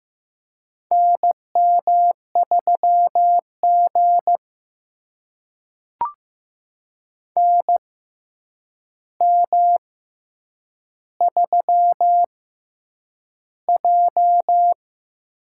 Morse code, heard as a keyed tone, about 15 wpm, spaced as for 8 wpm, 700 Hz. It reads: NM3G NM3J
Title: USA Call Signs - 4 Characters - 6x Character Spacing - 15wpm